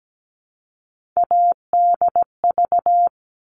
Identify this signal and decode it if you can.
Morse code, heard as a keyed tone, about 17 wpm, 700 Hz. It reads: ADV
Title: All Cut Numbers - 3-Digit 17wpm